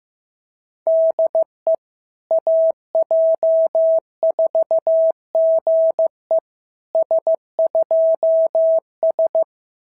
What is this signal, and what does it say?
Morse code, heard as a keyed tone, about 15 wpm, 650 Hz. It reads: DE AJ4GE S2S
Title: SOTA QSO Practice 15wpm